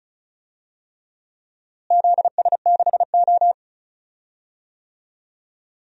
Morse code, heard as a keyed tone, about 34 wpm, 700 Hz. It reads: ZS6O
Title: DXCC-100 Call Signs - 4 Characters - Repeated 3x - 35wpm